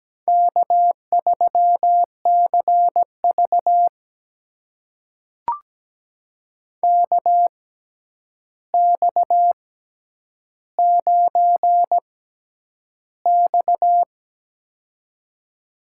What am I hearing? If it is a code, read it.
Morse code, heard as a keyed tone, about 17 wpm, spaced as for 9 wpm, 700 Hz. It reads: K3CV KX9X
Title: USA Call Signs - 4 Characters - 6x Character Spacing - 17wpm